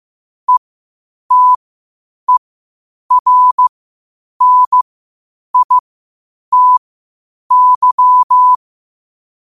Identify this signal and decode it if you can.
Morse code, heard as a keyed tone, about 15 wpm, spaced as for 8 wpm, 1000 Hz. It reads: ETERNITY
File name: qotdfiles / 8wpm